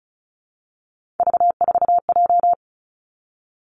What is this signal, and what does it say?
Morse code, heard as a keyed tone, about 36 wpm, 700 Hz. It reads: V4J
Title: Three Challenging Letter-Number Combinations 35wpm